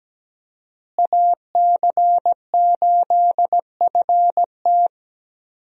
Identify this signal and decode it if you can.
Morse code, heard as a keyed tone, about 17 wpm, 700 Hz. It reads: AC8FT
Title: USA Call Signs - 5 Characters - Repeated 3x - 17wpm